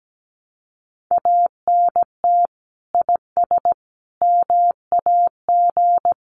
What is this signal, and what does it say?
Morse code, heard as a keyed tone, about 17 wpm, 700 Hz. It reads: ANT IS MAG
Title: Common QSO Phrases 17wpm